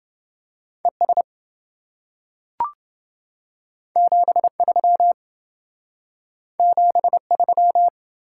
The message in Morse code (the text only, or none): ES 73 73